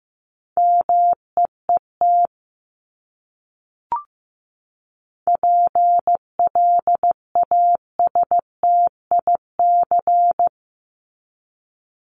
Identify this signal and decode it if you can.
Morse code, heard as a keyed tone, about 14 wpm, 700 Hz. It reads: MEET PLASTIC
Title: Top 1001-1250 Words 15wpm